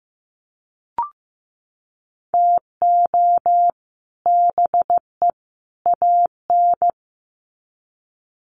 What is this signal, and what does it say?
Morse code, heard as a keyed tone, about 15 wpm, 700 Hz. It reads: TO BE AN
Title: Sets of 3 Words 15wpm